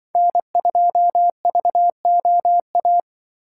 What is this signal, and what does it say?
Morse code, heard as a keyed tone, about 23 wpm, 700 Hz. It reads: N2VOA
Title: Speed Racing - USA 1x3 Call Signs 20wpm